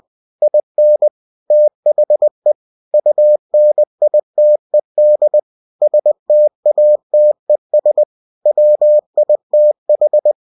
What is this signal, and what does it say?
Morse code, heard as a keyed tone, about 20 wpm, 600 Hz. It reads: IN THE UNITED STATES WITH
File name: cw-books / winnie-the-pooh / winnie-20wpm-page-0112